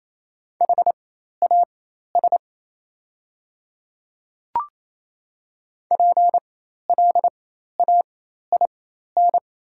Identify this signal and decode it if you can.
Morse code, heard as a keyed tone, about 28 wpm, spaced as for 13 wpm, 700 Hz. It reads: HAS PLAIN
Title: 4x Character Spacing - Top 500 Words - 28wpm